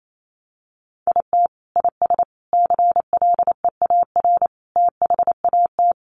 Morse code, heard as a keyed tone, about 28 wpm, 700 Hz. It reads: IT IS CLEAR THAT
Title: Sets of 5 Words Encore 28wpm